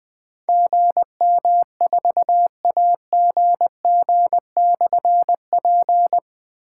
Morse code, heard as a keyed tone, about 20 wpm, 700 Hz. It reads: GM4AGG/P
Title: Difficult Call Signs 20wpm